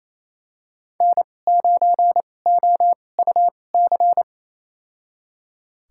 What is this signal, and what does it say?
Morse code, heard as a keyed tone, about 28 wpm, spaced as for 20 wpm, 700 Hz. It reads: N9OUC
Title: USA Call Signs - 5 Characters - 2x Character Spacing - 28wpm